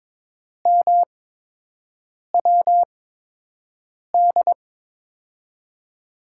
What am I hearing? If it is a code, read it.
Morse code, heard as a keyed tone, about 22 wpm, spaced as for 6 wpm, 700 Hz. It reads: MWD
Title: USA Call Signs - 8x Character Spacing - 22wpm